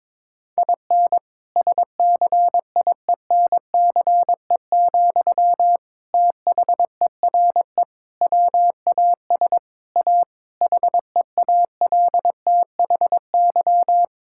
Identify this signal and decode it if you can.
Morse code, heard as a keyed tone, about 22 wpm, 700 Hz. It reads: IN SCIENCE, THERE WAS A HEALTHY
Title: Sentences from Top 1000 Words - Review Part 5 - 22wpm